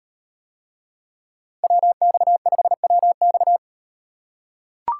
Morse code, heard as a keyed tone, about 38 wpm, 700 Hz. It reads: WX5WX